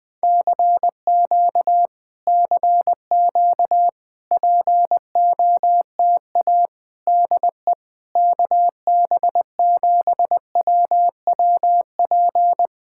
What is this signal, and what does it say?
Morse code, heard as a keyed tone, about 20 wpm, 700 Hz. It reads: CQ CQ POTA DE KB7WWP
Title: Speed-Racing Common QSO Phrases Encore - 17wpm